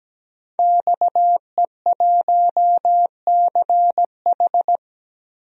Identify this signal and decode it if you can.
Morse code, heard as a keyed tone, about 17 wpm, 700 Hz. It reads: XE1CH
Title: DXCC-100 Call Signs - 5 Characters - 17wpm